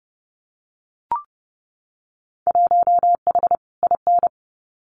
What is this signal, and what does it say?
Morse code, heard as a keyed tone, about 30 wpm, 700 Hz. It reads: 1H IN
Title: Winter Field Day Exchanges 30wpm